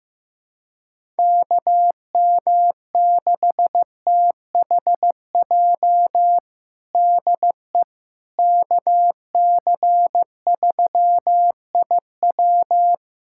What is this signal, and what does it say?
Morse code, heard as a keyed tone, about 16 wpm, 700 Hz. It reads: KM6THJ DE KC3IW
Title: USA Callsign DE Callsign 15wpm